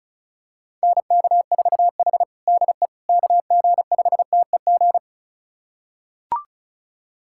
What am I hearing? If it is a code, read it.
Morse code, heard as a keyed tone, about 34 wpm, 700 Hz. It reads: NK4H DE KG5TEG
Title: USA Callsign DE Callsign 35wpm